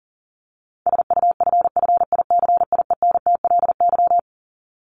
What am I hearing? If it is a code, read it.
Morse code, heard as a keyed tone, about 40 wpm, 700 Hz. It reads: SUFFICIENTLY
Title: LY Suffix - Top 100 Words with Base Words 40wpm